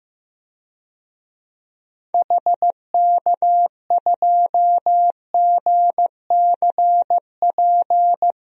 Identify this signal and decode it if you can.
Morse code, heard as a keyed tone, about 16 wpm, 700 Hz. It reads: HK2GCP
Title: DXCC-100 Call Signs - 6 Characters - 15wpm